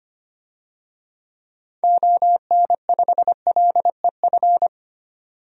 Morse code, heard as a keyed tone, about 25 wpm, 700 Hz. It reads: ON5LEF
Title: DXCC-100 Call Signs - 6 Characters - 25wpm